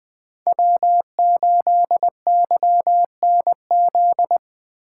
Morse code, heard as a keyed tone, about 20 wpm, 700 Hz. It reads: W8YNZ